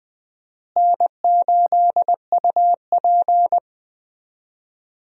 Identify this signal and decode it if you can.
Morse code, heard as a keyed tone, about 21 wpm, 700 Hz. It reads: N8UP
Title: USA Call Signs - 4 Characters - Repeated 3x - 20wpm